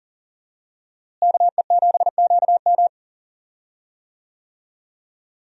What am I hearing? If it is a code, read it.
Morse code, heard as a keyed tone, about 40 wpm, 700 Hz. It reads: KE7QM
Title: USA Call Signs - 5 Characters - Repeated 2x - 40wpm